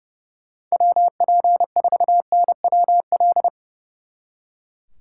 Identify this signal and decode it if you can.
Morse code, heard as a keyed tone, about 36 wpm, 700 Hz. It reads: WP4NWL
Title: Speed Racing - USA 2x3 Call Signs 30wpm